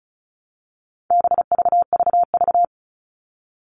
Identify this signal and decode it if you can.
Morse code, heard as a keyed tone, about 34 wpm, 700 Hz. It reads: BVVV
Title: All Cut Numbers - 4-Digit 35wpm